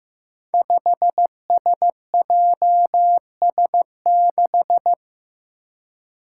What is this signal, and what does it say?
Morse code, heard as a keyed tone, about 15 wpm, 700 Hz. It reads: 5SJS6
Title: Five Challenging Letter-Number Combinations 15wpm